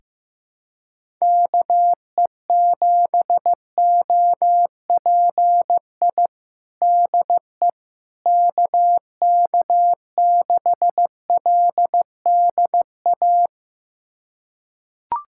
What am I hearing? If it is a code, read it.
Morse code, heard as a keyed tone, about 15 wpm, 700 Hz. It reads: KE7OPI DE KK6LDA
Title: USA Callsign DE Callsign - 6 Characters - 15wpm